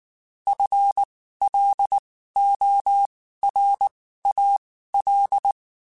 Morse code, heard as a keyed tone, about 19 wpm, spaced as for 14 wpm, 800 Hz. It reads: FLORAL
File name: Spaced / FL Runs sp mp3s / FL Runs sp @18